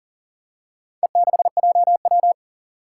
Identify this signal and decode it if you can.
Morse code, heard as a keyed tone, about 42 wpm, 700 Hz. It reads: E6JW
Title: Four Challenging Letter-Number Combinations 40wpm